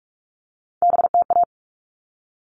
Morse code, heard as a keyed tone, about 44 wpm, 700 Hz. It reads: BTA
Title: All Cut Numbers - 3-Digit 45wpm